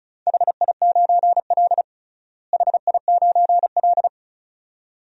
Morse code, heard as a keyed tone, about 34 wpm, 700 Hz. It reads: HI9L HI9L
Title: DXCC-100 Call Signs - 4 Characters - Repeated 2x - 35wpm